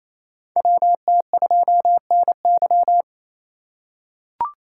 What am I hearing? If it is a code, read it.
Morse code, heard as a keyed tone, about 28 wpm, 700 Hz. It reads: WT2NY
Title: USA Call Signs - 5 Characters - 2x Character Spacing - 28wpm